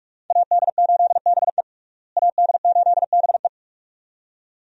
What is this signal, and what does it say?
Morse code, heard as a keyed tone, about 47 wpm, 700 Hz. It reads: AD8BE AD8BE
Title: USA Call Signs - 5 Characters - Repeated 2x - 45wpm